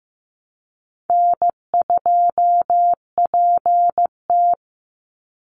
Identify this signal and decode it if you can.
Morse code, heard as a keyed tone, about 15 wpm, 700 Hz. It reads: N2PT
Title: USA Call Signs 15wpm